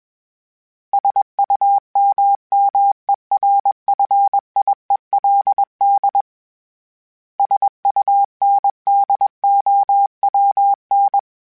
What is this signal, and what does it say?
Morse code, heard as a keyed tone, about 21 wpm, 800 Hz. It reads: SUMMERFIELD SUNDOWN
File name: fromSMs / FromSMs mp3s / Cities@20